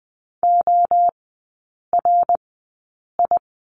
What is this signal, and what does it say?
Morse code, heard as a keyed tone, about 20 wpm, spaced as for 8 wpm, 700 Hz. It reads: ORI
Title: 053 Lesson 10e - Letter r - Three Characters 20wpm